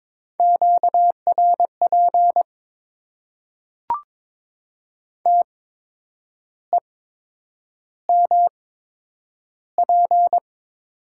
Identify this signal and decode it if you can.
Morse code, heard as a keyed tone, about 21 wpm, spaced as for 8 wpm, 700 Hz. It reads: QRP TEMP
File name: Top 90 QSO Elements - 8x Character Spacing - 22wpm